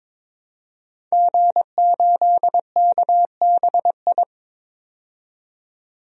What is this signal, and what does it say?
Morse code, heard as a keyed tone, about 22 wpm, 700 Hz. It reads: G8KBI
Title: DXCC-100 Call Signs - 5 Characters - 22wpm